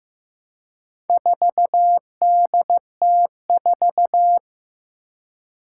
Morse code, heard as a keyed tone, about 15 wpm, 700 Hz. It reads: 4DT4
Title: Four Challenging Letter-Number Combinations 15wpm